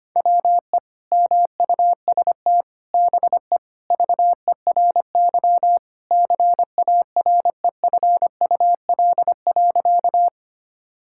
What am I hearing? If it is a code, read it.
Morse code, heard as a keyed tone, about 25 wpm, 700 Hz. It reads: WE MUST BE VERY CAREFUL.
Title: Sentences from Top 1600 Words 25wpm